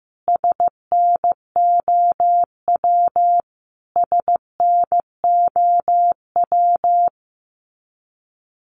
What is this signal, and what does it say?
Morse code, heard as a keyed tone, about 15 wpm, 700 Hz. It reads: SNOW SNOW
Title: Top 500 Words Repeated 15wpm